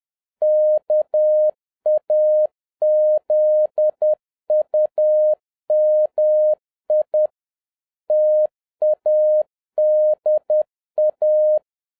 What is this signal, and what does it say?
Morse code, heard as a keyed tone, about 10 wpm, 610 Hz. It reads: KAZUMI TADA